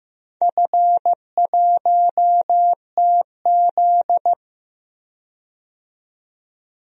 Morse code, heard as a keyed tone, about 15 wpm, 700 Hz. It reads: F1TZ
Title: DXCC-100 Call Signs - 4 Characters - 15wpm